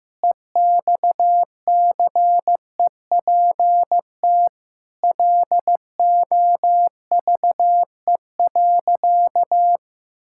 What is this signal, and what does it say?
Morse code, heard as a keyed tone, about 15 wpm, 700 Hz. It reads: EXCEPT LOVE.
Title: Sentences from Top 1600 Words 15wpm